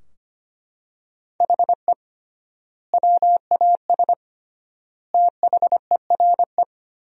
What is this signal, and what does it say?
Morse code, heard as a keyed tone, about 25 wpm, 700 Hz. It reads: HE WAS THERE
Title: 3x Word-Spacing - Sentences from Top 200 Words - 25wpm